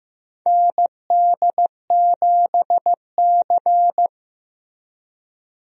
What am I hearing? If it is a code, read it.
Morse code, heard as a keyed tone, about 15 wpm, 700 Hz. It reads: ND7C
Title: USA Call Signs - 4 Characters - Repeated 3x - 15wpm